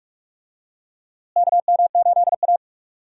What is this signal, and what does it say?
Morse code, heard as a keyed tone, about 46 wpm, 700 Hz. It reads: KM8A
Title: USA Call Signs - 4 Characters - Repeated 3x - 45wpm